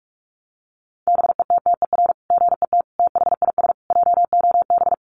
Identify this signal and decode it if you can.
Morse code, heard as a keyed tone, about 44 wpm, 700 Hz. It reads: BETTER GET THIS JOB
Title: Sentences from Top 600 Words 45wpm